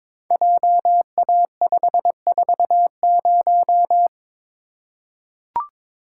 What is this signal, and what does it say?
Morse code, heard as a keyed tone, about 22 wpm, 700 Hz. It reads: JA540
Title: Code Groups - 5 Letter:Numbers - 22wpm